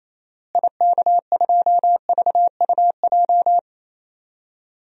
Morse code, heard as a keyed tone, about 28 wpm, 700 Hz. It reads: IK2VUJ
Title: DXCC-100 Call Signs - 6 Characters - Repeated 3x - 28wpm